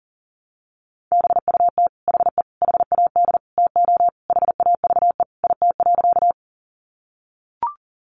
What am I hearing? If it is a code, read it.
Morse code, heard as a keyed tone, about 40 wpm, 700 Hz. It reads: BUT HE HAD TO HAVE IT.
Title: Sentences from Top 100 Words 40wpm